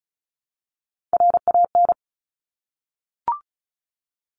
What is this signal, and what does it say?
Morse code, heard as a keyed tone, about 34 wpm, 700 Hz. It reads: RAN
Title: Top 500 Words 35wpm